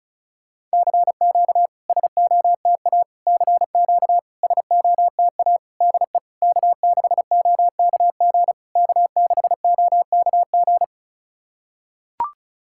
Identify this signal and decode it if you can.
Morse code, heard as a keyed tone, about 36 wpm, 660 Hz. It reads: CQ SOTA CQ SOTA DE K6OKG K6OKG
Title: SOTA QSO Practice 35wpm